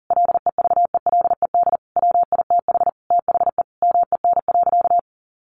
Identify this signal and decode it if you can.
Morse code, heard as a keyed tone, about 40 wpm, 700 Hz. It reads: LEVELED WITH THE MEN.
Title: A Sound Of Thunder 40wpm